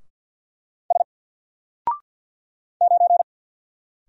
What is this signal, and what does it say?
Morse code, heard as a keyed tone, about 50 wpm, 700 Hz. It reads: S 9
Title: Single Character (Letter:Number:Punctuation:Prosign) 50wpm